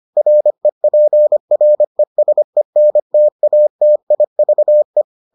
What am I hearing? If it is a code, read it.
Morse code, heard as a keyed tone, about 25 wpm, 600 Hz. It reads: REPRESENTATIVE